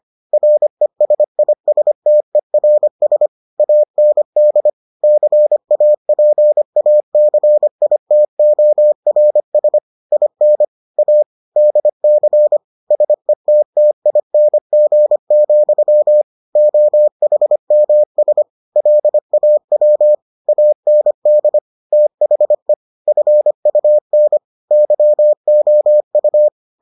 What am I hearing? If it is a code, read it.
Morse code, heard as a keyed tone, about 25 wpm, 600 Hz. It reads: RESISTERS AND CAPACITORS IN A DC SETTING, OHMS LAW AND THE FUN YOU